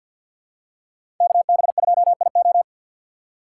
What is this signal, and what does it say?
Morse code, heard as a keyed tone, about 49 wpm, 700 Hz. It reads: KB2IO
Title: USA Call Signs - 5 Characters - 3x Character Spacing - 50wpm